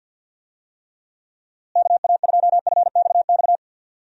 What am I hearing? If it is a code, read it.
Morse code, heard as a keyed tone, about 46 wpm, 700 Hz. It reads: KA2FXX
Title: USA Call Signs - 6 Characters - Repeated 3x - 50wpm